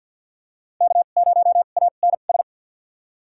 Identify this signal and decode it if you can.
Morse code, heard as a keyed tone, about 49 wpm, spaced as for 35 wpm, 700 Hz. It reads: K0ANS
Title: USA Call Signs - 5 Characters - 2x Character Spacing - 50wpm